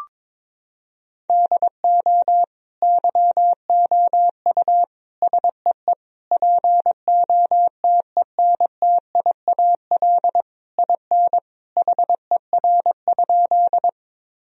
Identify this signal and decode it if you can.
Morse code, heard as a keyed tone, about 21 wpm, 700 Hz. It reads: DO YOU SEE POTENTIAL IN HER?
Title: Sentences from Top 1000 Words - Review Part 4 - 22wpm